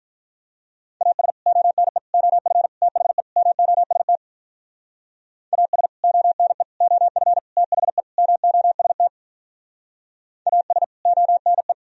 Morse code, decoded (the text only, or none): AS ONE OF THE MOST AS ONE OF THE MOST AS ONE